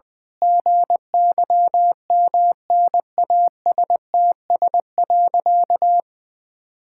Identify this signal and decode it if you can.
Morse code, heard as a keyed tone, about 19 wpm, 700 Hz. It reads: GYMNASTS.
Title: Pangram Challenge v2 - 20wpm